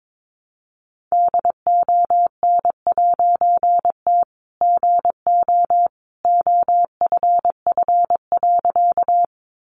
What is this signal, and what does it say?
Morse code, heard as a keyed tone, about 22 wpm, 700 Hz. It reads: DON'T GO OFF.
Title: A Sound Of Thunder 22wpm